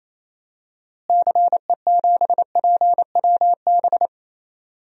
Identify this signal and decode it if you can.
Morse code, heard as a keyed tone, about 28 wpm, 700 Hz. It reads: CE7PWB
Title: DXCC-100 Call Signs - 6 Characters - Repeated 2x - 28wpm